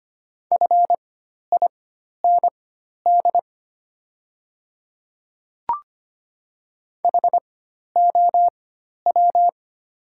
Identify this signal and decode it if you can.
Morse code, heard as a keyed tone, about 25 wpm, spaced as for 12 wpm, 690 Hz. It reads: FIND HOW
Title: 4x Character Spacing - Top 100 Words - 25wpm